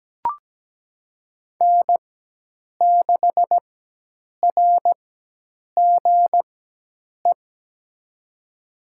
Morse code, heard as a keyed tone, about 17 wpm, spaced as for 8 wpm, 700 Hz. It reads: N6RGE
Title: USA Call Signs - 4x Character Spacing - 17wpm